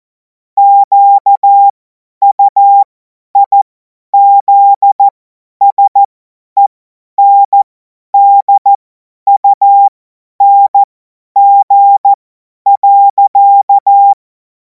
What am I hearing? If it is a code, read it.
Morse code, heard as a keyed tone, about 14 wpm, spaced as for 10 wpm, 800 Hz. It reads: QUIZSENDUNG.
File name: Morsekurs / Witze / Witz 12. Mai 2026 50 F70